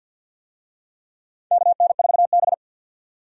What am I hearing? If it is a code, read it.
Morse code, heard as a keyed tone, about 50 wpm, 700 Hz. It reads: KN4B
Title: USA Call Signs - 4 Characters - 50wpm